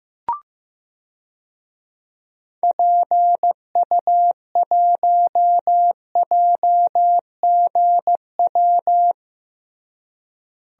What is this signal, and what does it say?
Morse code, heard as a keyed tone, about 16 wpm, 700 Hz. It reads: PU1JGW
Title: DXCC-100 Call Signs - 6 Characters - Repeated 2x - 15wpm